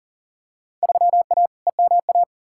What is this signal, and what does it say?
Morse code, heard as a keyed tone, about 37 wpm, 700 Hz. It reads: 3A EMA
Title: Field Day Exchanges - 40wpm